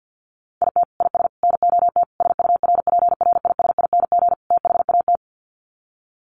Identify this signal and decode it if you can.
Morse code, heard as a keyed tone, about 48 wpm, 700 Hz. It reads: IT IS NOT SURPRISING THAT
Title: Sets of 5 Words 50wpm